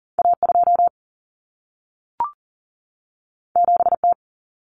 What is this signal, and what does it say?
Morse code, heard as a keyed tone, about 40 wpm, 700 Hz. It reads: A2 7T
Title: Common Cut Numbers - 2-Digit 40wpm